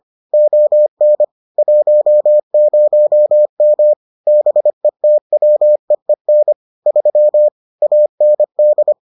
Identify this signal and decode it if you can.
Morse code, heard as a keyed tone, about 25 wpm, 600 Hz. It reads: ON 10M BETWEEN 3 AND